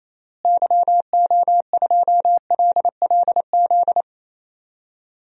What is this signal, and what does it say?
Morse code, heard as a keyed tone, about 27 wpm, 700 Hz. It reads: YO2LLZ
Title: DXCC-100 Call Signs - 6 Characters - Repeated 2x - 28wpm